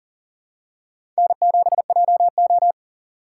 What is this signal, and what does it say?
Morse code, heard as a keyed tone, about 40 wpm, 700 Hz. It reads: N7JO
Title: USA Call Signs - 4 Characters - 2x Character Spacing - 40wpm